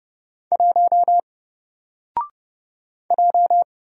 Morse code, heard as a keyed tone, about 30 wpm, 700 Hz. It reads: 1 J
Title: Single Character (Letter:Number:Punctuation:Prosign) 30wpm